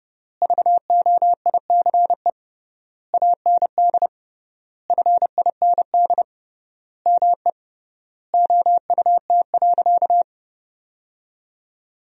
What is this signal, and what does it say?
Morse code, heard as a keyed tone, about 29 wpm, 700 Hz. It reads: VOICE AND FIND ME OUT.
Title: 3x Word-Spacing - Sentences from Top 500 Words - 30wpm